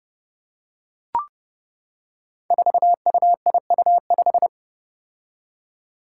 Morse code, heard as a keyed tone, about 30 wpm, 700 Hz. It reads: 4UIU5
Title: Five Challenging Letter-Number Combinations 30wpm